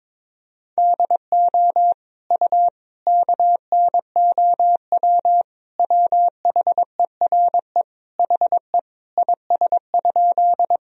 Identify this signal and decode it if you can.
Morse code, heard as a keyed tone, about 22 wpm, 700 Hz. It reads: DO U KNOW WHERE HE IS?
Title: Sentences from Top 300 Words and Abbreviations 22wpm